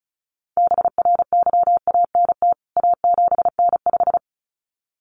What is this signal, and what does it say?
Morse code, heard as a keyed tone, about 34 wpm, 700 Hz. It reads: BRYANT A7N5
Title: CWT Exchange 35wpm